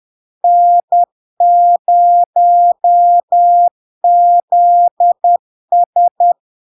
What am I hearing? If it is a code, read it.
Morse code, heard as a keyed tone, about 10 wpm, 700 Hz. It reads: N0ZS
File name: LARK-FD-2022-Log-in-CW-10wpm-700-Hz